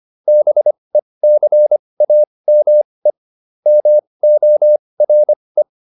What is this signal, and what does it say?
Morse code, heard as a keyed tone, about 25 wpm, spaced as for 20 wpm, 600 Hz. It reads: BECAME MORE
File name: SS315 20